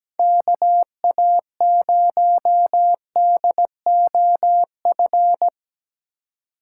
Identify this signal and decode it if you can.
Morse code, heard as a keyed tone, about 17 wpm, 700 Hz. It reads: KA0DOF
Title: USA Call Signs - 3x Character Spacing - 17wpm